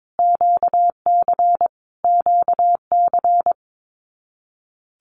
Morse code, heard as a keyed tone, about 22 wpm, 700 Hz. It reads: QC QC
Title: Canadian Provinces and Territories - Repeated 3x - 22wpm